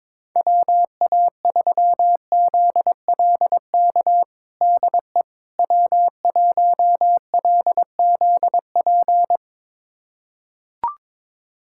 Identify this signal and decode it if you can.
Morse code, heard as a keyed tone, about 22 wpm, 700 Hz. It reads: WA3ZLK DE W1LZP
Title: USA Callsign DE Callsign 22wpm